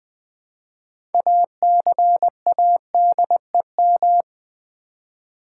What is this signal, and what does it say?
Morse code, heard as a keyed tone, about 20 wpm, 690 Hz. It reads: ACADEM
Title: Build A Word - 3 Letters At A Time - 8-Letter Words - 20wpm